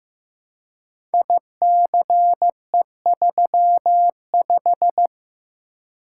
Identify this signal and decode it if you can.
Morse code, heard as a keyed tone, about 15 wpm, 700 Hz. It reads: ICE35
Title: Code Groups - 5 Letter:Numbers - 15wpm